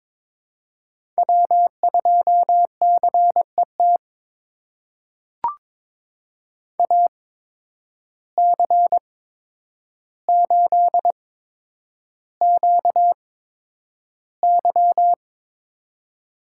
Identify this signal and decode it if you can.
Morse code, heard as a keyed tone, about 23 wpm, spaced as for 10 wpm, 700 Hz. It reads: W2CET AC8QY
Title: USA Call Signs - 8x Character Spacing - 22wpm